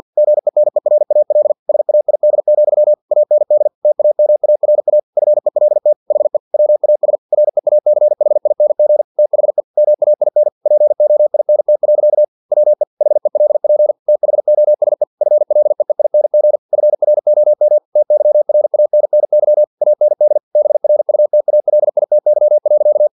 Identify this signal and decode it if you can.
Morse code, heard as a keyed tone, about 50 wpm, 600 Hz. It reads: GENERAL SAID, AND TAMARA FELT HE WAS REACHING THE MAIN POINT. WE HELP THOSE FLEEING FROM TYRANNY AND BRUTALITY.
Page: Never 49wpm